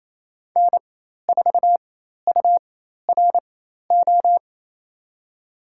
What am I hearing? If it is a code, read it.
Morse code, heard as a keyed tone, about 27 wpm, spaced as for 13 wpm, 700 Hz. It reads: N4URO
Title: USA Call Signs - 5 Characters - 4x Character Spacing - 28wpm